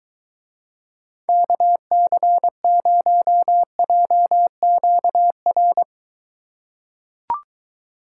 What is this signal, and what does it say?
Morse code, heard as a keyed tone, about 23 wpm, 700 Hz. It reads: KC0JQR